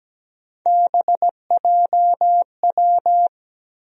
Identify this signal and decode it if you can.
Morse code, heard as a keyed tone, about 17 wpm, 700 Hz. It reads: BJW
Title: USA Call Sign Postfixes - 3 Letters 17wpm